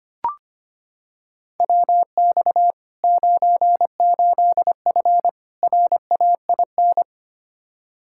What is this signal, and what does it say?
Morse code, heard as a keyed tone, about 25 wpm, 700 Hz. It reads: WX 98F RAIN
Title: Practice QSO - Exchange 2 - Info - 25wpm